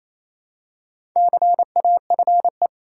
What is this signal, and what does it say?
Morse code, heard as a keyed tone, about 27 wpm, 690 Hz. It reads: CAFE
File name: Easy Words - Speed-Racing with Hints - 22wpm